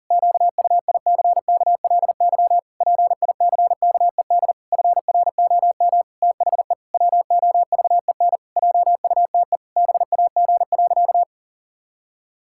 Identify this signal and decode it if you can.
Morse code, heard as a keyed tone, about 37 wpm, 700 Hz. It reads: QUICKLY PICKED FROM THE WOVEN JUTE BAG.
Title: Pangram Challenge v2 - 40wpm